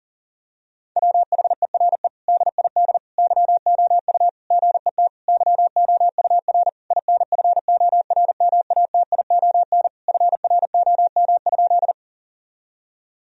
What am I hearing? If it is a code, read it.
Morse code, heard as a keyed tone, about 40 wpm, 700 Hz. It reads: WHERE DID YOU GET YOUR INFORMATION FROM?
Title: Sentences from Top 1000 Words - Review Part 3 - 40wpm